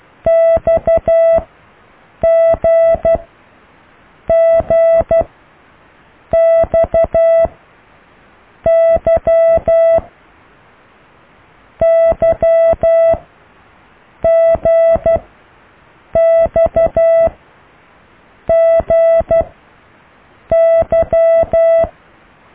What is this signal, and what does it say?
Morse code, heard as a keyed tone, about 12 wpm, spaced as for 6 wpm, 660 Hz. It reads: XGGXY YGXGY